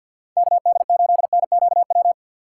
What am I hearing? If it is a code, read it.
Morse code, heard as a keyed tone, about 50 wpm, 700 Hz. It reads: KD8NQW
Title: USA Call Signs - 2x Character Spacing - 50wpm